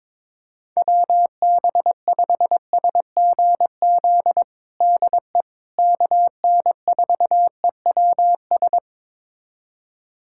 Morse code, heard as a keyed tone, about 21 wpm, 700 Hz. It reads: WB5SGZ DE KN4EWS
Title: USA Callsign DE Callsign 22wpm